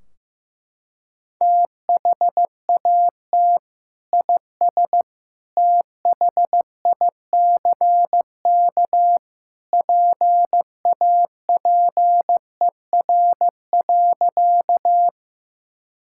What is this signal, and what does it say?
Morse code, heard as a keyed tone, about 15 wpm, 700 Hz. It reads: THAT IS THICK PAPER.
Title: Sentences from Top 1600 Words 15wpm